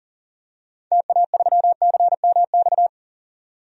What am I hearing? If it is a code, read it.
Morse code, heard as a keyed tone, about 40 wpm, 700 Hz. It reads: TA3CMX
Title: DXCC-100 Call Signs - 6 Characters - 40wpm